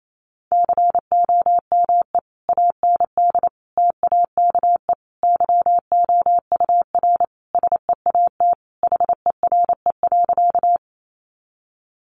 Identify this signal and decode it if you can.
Morse code, heard as a keyed tone, about 28 wpm, 700 Hz. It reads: COME AND TAKE YOUR SEAT HERE.
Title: Sentences from Top 1500 Words 28wpm